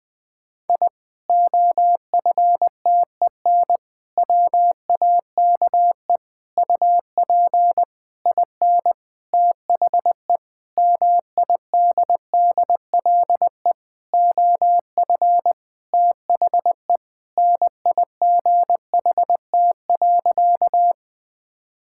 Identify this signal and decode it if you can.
Morse code, heard as a keyed tone, about 19 wpm, 700 Hz. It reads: I OFTEN WAKE UP IN THE MIDDLE OF THE NIGHT.
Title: Sentences with 8-Word Phrases - 20wpm